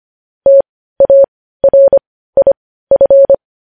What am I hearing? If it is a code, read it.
Morse code, heard as a keyed tone, about 25 wpm, spaced as for 15 wpm, 550 Hz. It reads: TARIF
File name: podcast / News headlines-1x-25x15